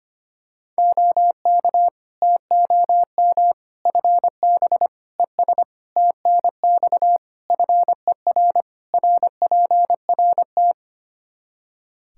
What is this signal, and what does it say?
Morse code, heard as a keyed tone, about 25 wpm, 700 Hz. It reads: OK TOM FB ES TNX FER RPRT
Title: Practice QSO - Exchange 2 - Info - 25wpm